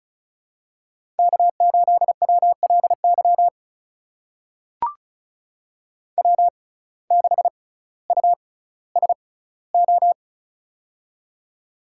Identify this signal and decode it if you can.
Morse code, heard as a keyed tone, about 36 wpm, spaced as for 18 wpm, 700 Hz. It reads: K8WLY W6USO
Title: USA Call Signs - 5 Characters - 6x Character Spacing - 35wpm